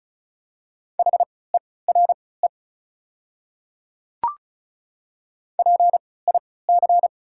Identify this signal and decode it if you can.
Morse code, heard as a keyed tone, about 36 wpm, spaced as for 20 wpm, 700 Hz. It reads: HERE PIC
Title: 3x Character Spacing - Top 200 Words - 35wpm